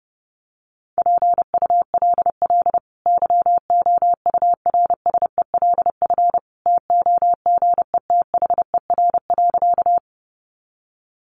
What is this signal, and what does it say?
Morse code, heard as a keyed tone, about 28 wpm, 700 Hz. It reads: PULL YOURSELF TOGETHER.
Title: Sentences from Top 600 Words 30wpm